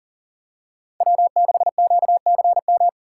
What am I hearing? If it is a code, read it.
Morse code, heard as a keyed tone, about 40 wpm, 700 Hz. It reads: W6QCM
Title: USA Call Signs - 5 Characters - 6x Character Spacing - 40wpm